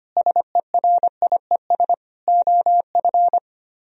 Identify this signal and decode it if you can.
Morse code, heard as a keyed tone, about 26 wpm, 700 Hz. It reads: SERIES OF
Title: Speed-Racing - Sets of 3 Words 25wpm